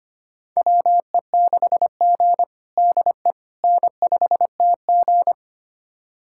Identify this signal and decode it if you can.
Morse code, heard as a keyed tone, about 25 wpm, 700 Hz. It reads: WE6G DE N5TG
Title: USA Callsign DE Callsign - 4 Characters - 25wpm